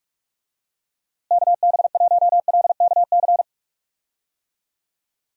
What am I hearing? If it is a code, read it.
Morse code, heard as a keyed tone, about 41 wpm, 700 Hz. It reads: KB1LKC
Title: USA Call Signs - 6 Characters - Repeated 2x - 45wpm